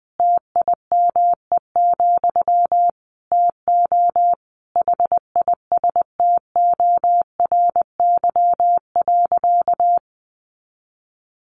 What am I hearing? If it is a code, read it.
Morse code, heard as a keyed tone, about 20 wpm, 700 Hz. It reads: TIME, TO HISTORY.
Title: A Sound Of Thunder 20wpm